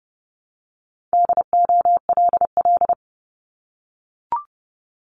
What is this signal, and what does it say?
Morse code, heard as a keyed tone, about 31 wpm, 700 Hz. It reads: DOLL
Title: Multi-letter Phonemes Repeated 3x 30wpm